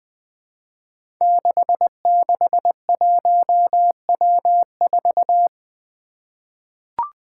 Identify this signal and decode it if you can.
Morse code, heard as a keyed tone, about 20 wpm, 700 Hz. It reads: 661W4
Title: Five Challenging Letter-Number Combinations 20wpm